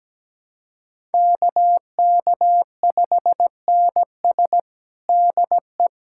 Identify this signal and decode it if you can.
Morse code, heard as a keyed tone, about 17 wpm, 700 Hz. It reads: KK5NS DE
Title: USA Callsign DE Callsign 17wpm